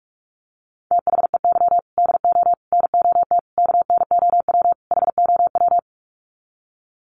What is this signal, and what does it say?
Morse code, heard as a keyed tone, about 44 wpm, 700 Hz. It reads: THEY DO NOT KNOW HOW
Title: Sets of 5 Words Encore2 45wpm